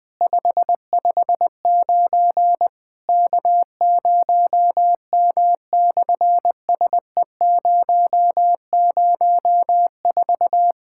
Morse code, heard as a keyed tone, about 18 wpm, 700 Hz. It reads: 559 K0M/SE004
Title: SOTA QSO Practice 20wpm